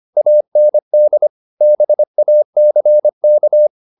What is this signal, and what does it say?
Morse code, heard as a keyed tone, about 25 wpm, 600 Hz. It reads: AND BACK